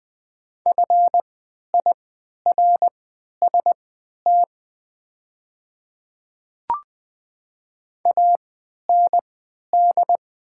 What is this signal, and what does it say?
Morse code, heard as a keyed tone, about 20 wpm, spaced as for 11 wpm, 700 Hz. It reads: FIRST AND
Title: 3x Character Spacing - Top 100 Words - 20wpm